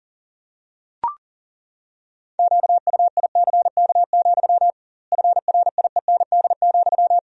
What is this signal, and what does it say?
Morse code, heard as a keyed tone, about 40 wpm, 700 Hz. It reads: QUICK, FRIEND,
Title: 2x Word-Spacing - Sentences from Top 500 Words - 40wpm